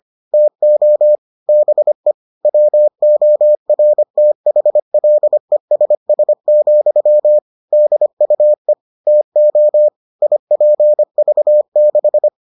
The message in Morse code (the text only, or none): TO BE WORTHLESS, DUE TO IPV6